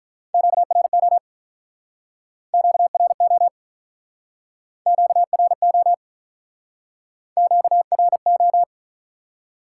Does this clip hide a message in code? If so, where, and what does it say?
Yes: Morse code from 0.3 to 8.6 s, heard as a keyed tone, about 47 wpm, 700 Hz. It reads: QRO QRO QRO QRO